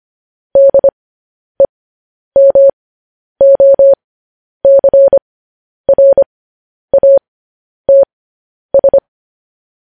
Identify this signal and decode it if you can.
Morse code, heard as a keyed tone, about 25 wpm, spaced as for 10 wpm, 550 Hz. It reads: DEMOCRATS